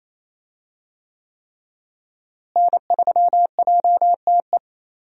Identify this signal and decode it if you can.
Morse code, heard as a keyed tone, about 28 wpm, 700 Hz. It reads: N3JTE
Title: USA Call Signs - 5 Characters - Repeated 3x - 28wpm